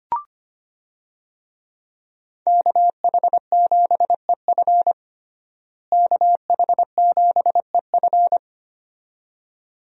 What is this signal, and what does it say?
Morse code, heard as a keyed tone, about 25 wpm, 700 Hz. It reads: KH7EF KH7EF
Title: USA Call Signs - 5 Characters - Repeated 2x - 25wpm